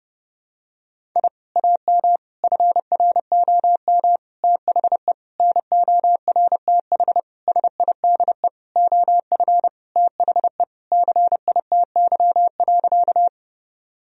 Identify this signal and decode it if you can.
Morse code, heard as a keyed tone, about 31 wpm, 700 Hz. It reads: I AM FROM THE NORTH SIDE OF THE CITY.
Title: Sentences from Top 1000 Words - Review Part 1 - 30wpm